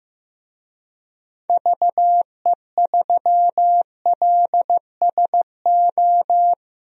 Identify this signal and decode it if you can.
Morse code, heard as a keyed tone, about 16 wpm, 700 Hz. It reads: VE3LSO
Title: DXCC-100 Call Signs - 6 Characters - Repeated 2x - 15wpm